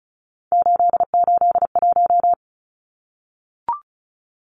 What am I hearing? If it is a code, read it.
Morse code, heard as a keyed tone, about 34 wpm, 700 Hz. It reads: 881
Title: Numbers - 3-Digit 35wpm